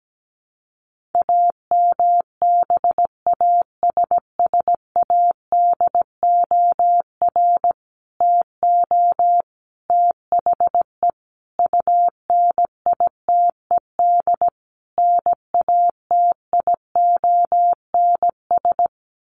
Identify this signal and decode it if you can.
Morse code, heard as a keyed tone, about 17 wpm, 700 Hz. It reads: AMBASSADOR TO THE UNITED NATIONS
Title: Sets of 5 Words Encore2 17wpm